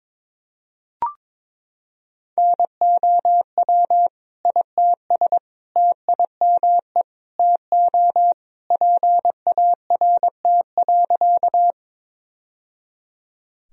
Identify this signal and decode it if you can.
Morse code, heard as a keyed tone, about 21 wpm, 700 Hz. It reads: NOW ITS TIME TO PART.
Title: Sentences from Top 1000 Words - Review Part 1 - 22wpm